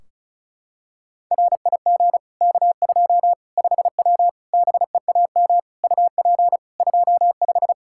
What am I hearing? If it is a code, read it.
Morse code, heard as a keyed tone, about 36 wpm, 700 Hz. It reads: RIG K2 5W BEAM UP 25
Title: Practice QSO - Exchange 2 - Info - 35wpm